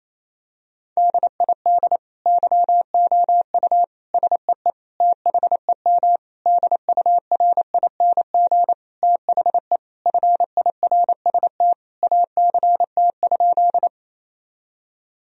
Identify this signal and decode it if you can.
Morse code, heard as a keyed tone, about 28 wpm, 700 Hz. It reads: DID YOU SEE THEM DURING THE FIRST ACT?
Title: Sentences from Top 1000 Words - Review Part 2 - 28wpm